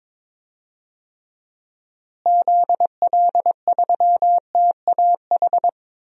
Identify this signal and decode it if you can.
Morse code, heard as a keyed tone, about 22 wpm, 700 Hz. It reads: ZL3TAH
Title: DXCC-100 Call Signs - 6 Characters - Repeated 3x - 22wpm